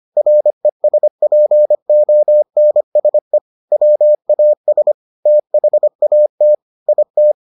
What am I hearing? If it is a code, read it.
Morse code, heard as a keyed tone, about 25 wpm, 600 Hz. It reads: RESPONSE WAS THAT IT